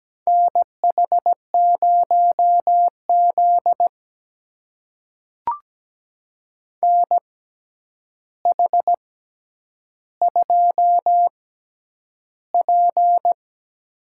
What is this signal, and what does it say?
Morse code, heard as a keyed tone, about 17 wpm, spaced as for 9 wpm, 700 Hz. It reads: NH0Z NH2P
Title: USA Call Signs - 4 Characters - 6x Character Spacing - 17wpm